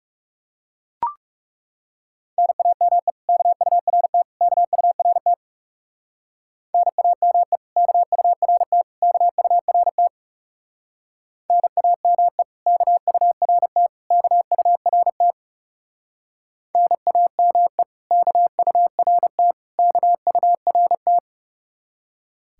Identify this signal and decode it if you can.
Morse code, heard as a keyed tone, about 36 wpm, 700 Hz. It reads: NAME KURT KURT NAME KURT KURT NAME KURT KURT NAME KURT KURT
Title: Speed-Racing Common QSO Phrases Encore - 30wpm